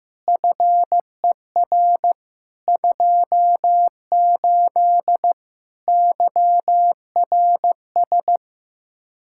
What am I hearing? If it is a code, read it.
Morse code, heard as a keyed tone, about 15 wpm, 700 Hz. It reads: FER 28 YRS
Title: Practice QSO - Exchange 3 - More Info - 15wpm